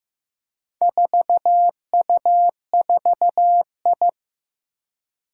Morse code, heard as a keyed tone, about 15 wpm, 700 Hz. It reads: 4U4I
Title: Four Challenging Letter-Number Combinations 15wpm